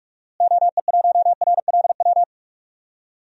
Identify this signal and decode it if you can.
Morse code, heard as a keyed tone, about 46 wpm, 700 Hz. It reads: OE1RLW
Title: DXCC-100 Call Signs - 6 Characters - Repeated 3x - 45wpm